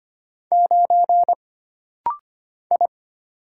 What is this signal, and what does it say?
Morse code, heard as a keyed tone, about 25 wpm, 700 Hz. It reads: 9 I
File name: Single Character (Letter:Number:Punctuation:Prosign) - Rapid-Fire - 25wpm